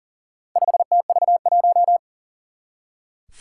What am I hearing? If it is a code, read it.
Morse code, heard as a keyed tone, about 39 wpm, 700 Hz. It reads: 5TV1
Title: Four Challenging Letter-Number Combinations 40wpm